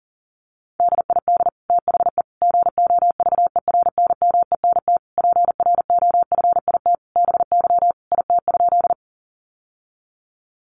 Morse code, heard as a keyed tone, about 40 wpm, 690 Hz. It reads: DID THE GOVERNMENT PROFIT BY IT?
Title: Sentences from Top 800 Words 40wpm